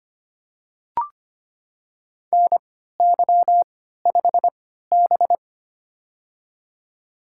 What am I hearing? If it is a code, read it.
Morse code, heard as a keyed tone, about 25 wpm, spaced as for 14 wpm, 700 Hz. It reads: NY5B
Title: USA Call Signs - 3x Character Spacing - 25wpm